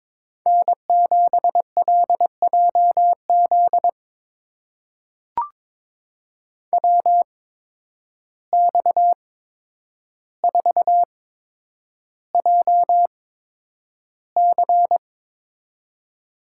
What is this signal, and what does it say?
Morse code, heard as a keyed tone, about 22 wpm, spaced as for 9 wpm, 700 Hz. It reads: N7LJZ WX4JC
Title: USA Call Signs - 5 Characters - 8x Character Spacing - 22wpm